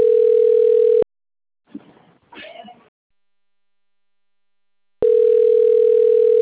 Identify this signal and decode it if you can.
Ringback tone (North America), 440 + 480 Hz.